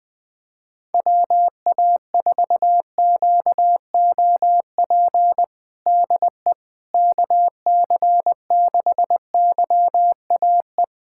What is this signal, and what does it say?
Morse code, heard as a keyed tone, about 20 wpm, 700 Hz. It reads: WA4QOP DE KC6YAE
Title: USA Callsign DE Callsign 20wpm